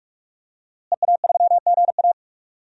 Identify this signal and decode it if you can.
Morse code, heard as a keyed tone, about 41 wpm, 700 Hz. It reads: EA3GA
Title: DXCC-100 Call Signs - 5 Characters - Repeated 3x - 45wpm